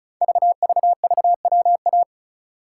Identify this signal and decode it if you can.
Morse code, heard as a keyed tone, about 36 wpm, 700 Hz. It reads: VVVWA
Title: Code Groups - 5 Letters - 35wpm